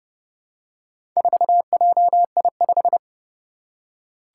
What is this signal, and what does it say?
Morse code, heard as a keyed tone, about 31 wpm, 700 Hz. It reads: 4JI5
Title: Four Challenging Letter-Number Combinations 30wpm